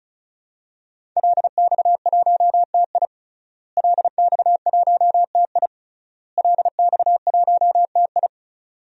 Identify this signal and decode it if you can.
Morse code, heard as a keyed tone, about 36 wpm, 700 Hz. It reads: LX1TI LX1TI LX1TI
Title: DXCC-100 Call Signs - 5 Characters - Repeated 3x - 35wpm